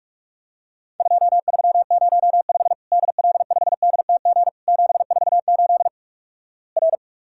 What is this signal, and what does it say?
Morse code, heard as a keyed tone, about 48 wpm, 700 Hz. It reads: 2305 DL5DTG 748 R
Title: Worked All Europe DX Contest - QTC Exchange - 45wpm